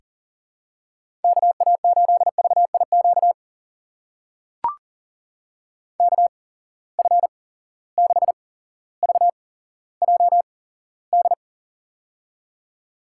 Morse code, heard as a keyed tone, about 40 wpm, spaced as for 17 wpm, 700 Hz. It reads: KA8VIQ KF6VJD
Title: USA Call Signs - 6 Characters - 8x Character Spacing - 40wpm